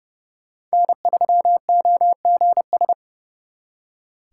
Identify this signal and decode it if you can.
Morse code, heard as a keyed tone, about 36 wpm, 700 Hz. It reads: N3OGS
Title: Speed Racing - USA 1x3 Call Signs 30wpm